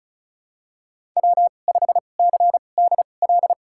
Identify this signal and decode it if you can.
Morse code, heard as a keyed tone, about 36 wpm, spaced as for 25 wpm, 700 Hz. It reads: W5CDL
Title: USA Call Signs - 5 Characters - 2x Character Spacing - 35wpm